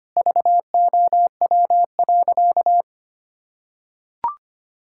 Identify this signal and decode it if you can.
Morse code, heard as a keyed tone, about 25 wpm, 700 Hz. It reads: VOW.
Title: Pangram Challenge v2 - 25wpm